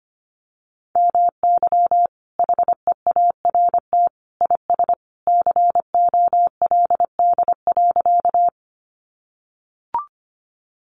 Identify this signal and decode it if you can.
Morse code, heard as a keyed tone, about 25 wpm, 700 Hz. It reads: MY HEART IS COLD.
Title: Sentences from Top 1200 Words 25wpm